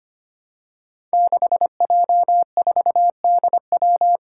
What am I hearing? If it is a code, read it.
Morse code, heard as a keyed tone, about 25 wpm, 700 Hz. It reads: 6J4DW
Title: Five Challenging Letter-Number Combinations 25wpm